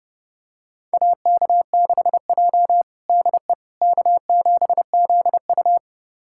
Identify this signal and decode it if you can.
Morse code, heard as a keyed tone, about 30 wpm, 700 Hz. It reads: AK6J DE K7ZU
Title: USA Callsign DE Callsign - 4 Characters - 30wpm